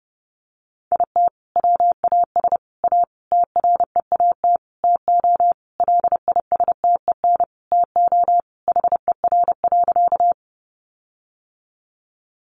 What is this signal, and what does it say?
Morse code, heard as a keyed tone, about 30 wpm, 700 Hz. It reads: IT WAS A TREAT TO LISTEN TO HER.
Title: Sentences from Top 1300 Words 30wpm